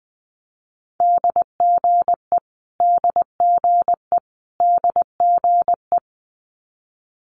Morse code, heard as a keyed tone, about 20 wpm, 700 Hz. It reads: DGE DGE DGE
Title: Multi-letter Phonemes Repeated 3x 20wpm